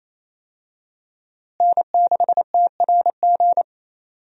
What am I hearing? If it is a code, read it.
Morse code, heard as a keyed tone, about 28 wpm, 700 Hz. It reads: N6TRG
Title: USA Call Signs - 5 Characters - Repeated 3x - 28wpm